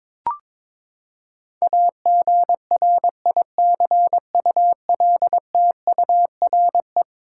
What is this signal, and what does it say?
Morse code, heard as a keyed tone, about 22 wpm, 700 Hz. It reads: AGRICULTURE
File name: practice / 22wpm / Eleven Letter Words - 22wpm